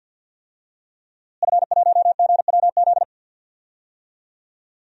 Morse code, heard as a keyed tone, about 47 wpm, 700 Hz. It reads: F1GWZ
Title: DXCC-100 Call Signs - 5 Characters - Repeated 2x - 50wpm